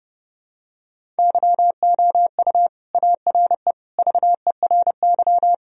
Morse code, heard as a keyed tone, about 30 wpm, 700 Hz. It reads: YOU ARE VERY
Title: Sentences from Top 1000 Words - Review Part 1 - 30wpm